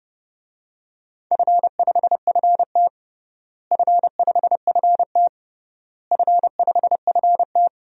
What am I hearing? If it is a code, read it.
Morse code, heard as a keyed tone, about 29 wpm, 700 Hz. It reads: F5FT F5FT F5FT
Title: DXCC-100 Call Signs - 4 Characters - Repeated 3x - 30wpm